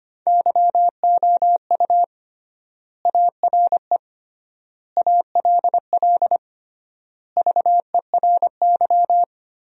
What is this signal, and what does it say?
Morse code, heard as a keyed tone, about 25 wpm, 700 Hz. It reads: YOU ARE ALL VERY
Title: 3x Word-Spacing - Sentences from Top 200 Words - 25wpm